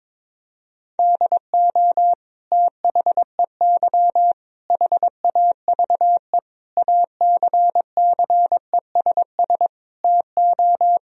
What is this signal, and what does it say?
Morse code, heard as a keyed tone, about 22 wpm, 700 Hz. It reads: DO THEY HAVE ACCESS TO
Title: Sentences from Top 1000 Words - Review Part 4 - 22wpm